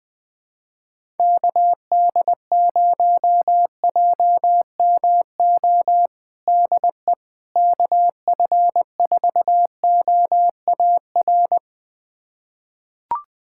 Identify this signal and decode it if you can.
Morse code, heard as a keyed tone, about 20 wpm, 700 Hz. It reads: KD0JMO DE KF4OAR
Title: USA Callsign DE Callsign - 6 Characters - 20wpm